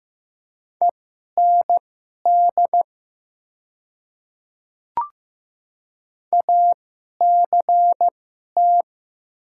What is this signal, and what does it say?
Morse code, heard as a keyed tone, about 15 wpm, spaced as for 11 wpm, 700 Hz. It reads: END ACT
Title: 2x Character Spacing - Top 500 Words - 15wpm